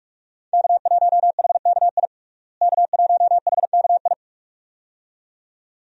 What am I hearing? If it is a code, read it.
Morse code, heard as a keyed tone, about 46 wpm, 700 Hz. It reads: K1HKI K1HKI
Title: USA Call Signs - 5 Characters - Repeated 2x - 45wpm